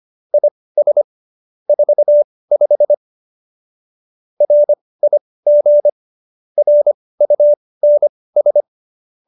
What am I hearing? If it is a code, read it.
Morse code, heard as a keyed tone, about 25 wpm, spaced as for 18 wpm, 590 Hz. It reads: IS 45 RIG RUNS